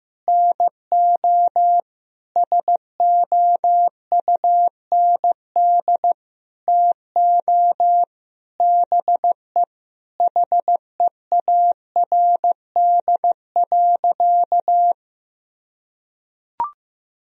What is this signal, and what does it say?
Morse code, heard as a keyed tone, about 15 wpm, 700 Hz. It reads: NO SOUND TO BE HEARD.
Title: Sentences from Top 1000 Words - Review Part 1 - 15wpm